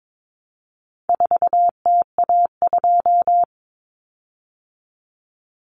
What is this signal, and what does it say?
Morse code, heard as a keyed tone, about 22 wpm, 700 Hz. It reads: 4TA2
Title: Common Cut Numbers - 4-Digit 22wpm